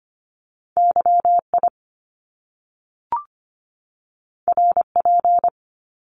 Morse code, heard as a keyed tone, about 25 wpm, 700 Hz. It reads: YI RP
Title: Top Two Letter Combinations in Words 25wpm